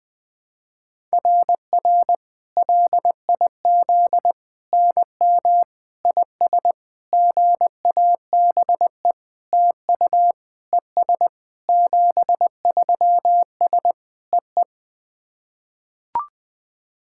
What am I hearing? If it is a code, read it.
Morse code, heard as a keyed tone, about 19 wpm, 700 Hz. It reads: RR LIZ NM IS GABE TU ES 73S EE
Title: SOTA QSO Practice 20wpm